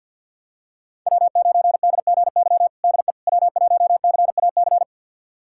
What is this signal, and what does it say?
Morse code, heard as a keyed tone, about 47 wpm, 700 Hz. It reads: W9DGY DE W1CAC
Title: USA Callsign DE Callsign - 5 Characters - 50wpm